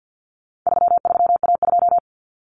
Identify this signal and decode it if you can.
Morse code, heard as a keyed tone, about 48 wpm, 700 Hz. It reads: 33A2
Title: Common Cut Numbers - 4-Digit 50wpm